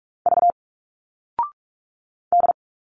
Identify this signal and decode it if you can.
Morse code, heard as a keyed tone, about 46 wpm, 700 Hz. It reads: V D
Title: Single Letters 45wpm